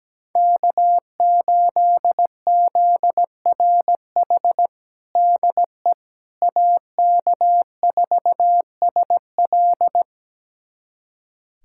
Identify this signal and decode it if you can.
Morse code, heard as a keyed tone, about 17 wpm, 700 Hz. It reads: K8ZRH DE AK4SL
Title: USA Callsign DE Callsign - 5 Characters - 17wpm